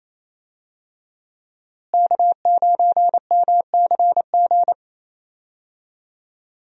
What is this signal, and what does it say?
Morse code, heard as a keyed tone, about 28 wpm, 700 Hz. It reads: K9MCG
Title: USA Call Signs - 5 Characters - Repeated 2x - 28wpm